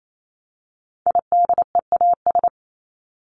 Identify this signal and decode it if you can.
Morse code, heard as a keyed tone, about 28 wpm, 700 Hz. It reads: IDEAS
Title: ES:S Suffix - Top 100 Words with Base Words 28wpm